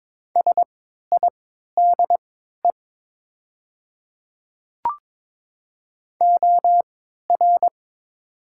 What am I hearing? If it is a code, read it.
Morse code, heard as a keyed tone, about 22 wpm, spaced as for 13 wpm, 700 Hz. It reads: SIDE OR
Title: 3x Character Spacing - Top 100 Words - 22wpm